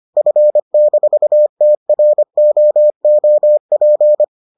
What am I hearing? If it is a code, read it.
Morse code, heard as a keyed tone, about 25 wpm, 600 Hz. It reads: F-TROOP